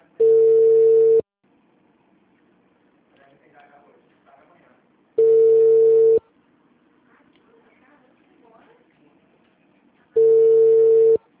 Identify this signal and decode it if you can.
Ringback tone (China, 450 Hz), with speech.